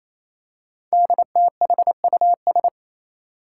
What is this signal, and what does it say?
Morse code, heard as a keyed tone, about 29 wpm, 700 Hz. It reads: DTHUS
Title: Five Challenging Letter-Number Combinations 28wpm